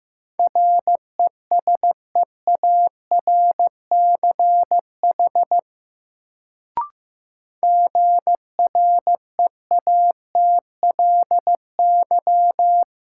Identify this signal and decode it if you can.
Morse code, heard as a keyed tone, about 15 wpm, 700 Hz. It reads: RESEARCH GREATLY
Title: Sentences from Top 1400 Words - New Word Drill Alt - 15wpm